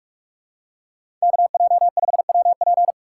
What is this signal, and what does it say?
Morse code, heard as a keyed tone, about 42 wpm, 700 Hz. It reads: KJ5WP
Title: USA Call Signs - 5 Characters - 4x Character Spacing - 45wpm